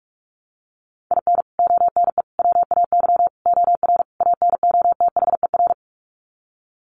Morse code, heard as a keyed tone, about 44 wpm, 700 Hz. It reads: IN ONE WAY OR ANOTHER
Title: Sets of 5 Words Encore 45wpm